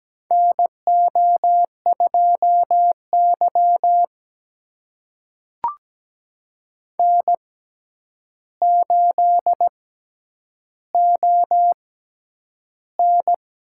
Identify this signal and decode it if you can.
Morse code, heard as a keyed tone, about 17 wpm, spaced as for 9 wpm, 700 Hz. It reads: NO2Y N8ON
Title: USA Call Signs - 6x Character Spacing - 17wpm